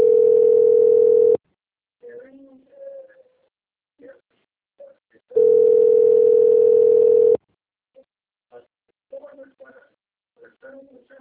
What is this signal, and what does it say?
Ringback tone (North America), 440 + 480 Hz, with speech.
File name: out- - - - - .988